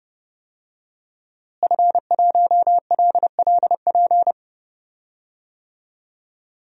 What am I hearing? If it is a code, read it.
Morse code, heard as a keyed tone, about 30 wpm, 700 Hz. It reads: F1LLP
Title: DXCC-100 Call Signs - 5 Characters - Repeated 2x - 30wpm